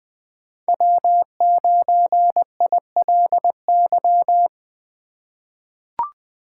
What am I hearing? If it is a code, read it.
Morse code, heard as a keyed tone, about 20 wpm, 700 Hz. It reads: W9ILY
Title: K1USN SST QSO Practice - 20wpm